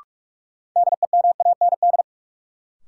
Morse code, heard as a keyed tone, about 46 wpm, 700 Hz. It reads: DEMAND
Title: Sentences from Top 800 Words - New Word Drill - 45wpm